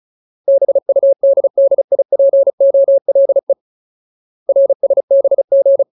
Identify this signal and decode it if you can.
Morse code, heard as a keyed tone, about 34 wpm, 550 Hz. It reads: BUDDIPOLE RSBG
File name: Ultimate Word List / 35WPM Ultimate